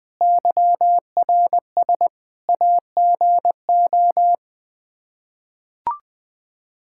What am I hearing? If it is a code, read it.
Morse code, heard as a keyed tone, about 21 wpm, 700 Hz. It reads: YRS AGO
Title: K1USN SST QSO Practice 20wpm